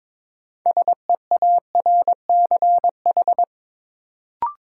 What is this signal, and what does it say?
Morse code, heard as a keyed tone, about 22 wpm, 700 Hz. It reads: SEARCH
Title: Sentences from Top 800 Words - New Word Drill - 22wpm